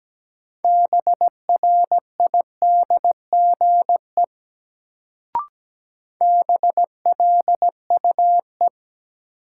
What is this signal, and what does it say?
Morse code, heard as a keyed tone, about 17 wpm, 700 Hz. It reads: BRIDGE BLUE
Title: Sentences from Top 1200 Words - New Word Drill Alt - 17wpm